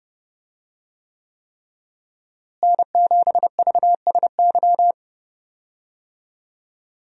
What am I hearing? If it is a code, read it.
Morse code, heard as a keyed tone, about 30 wpm, 700 Hz. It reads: N7VSY
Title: USA Call Signs - 5 Characters - Repeated 3x - 30wpm